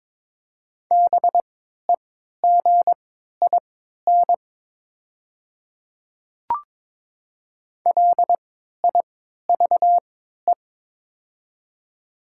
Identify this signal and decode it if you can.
Morse code, heard as a keyed tone, about 22 wpm, spaced as for 12 wpm, 700 Hz. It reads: BEGIN LIVE
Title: 3x Character Spacing - Top 300 Words - 22wpm